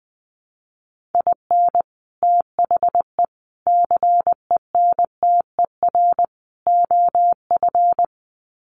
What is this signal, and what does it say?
Morse code, heard as a keyed tone, about 20 wpm, 700 Hz. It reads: IN THE CENTER OF
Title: Sets of 4 Words 20wpm